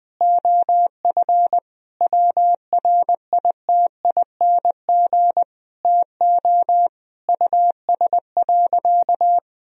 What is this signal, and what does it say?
Morse code, heard as a keyed tone, about 21 wpm, 700 Hz. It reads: OF WRITING TO US.
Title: Sentences from Top 1000 Words - Review Part 4 - 20wpm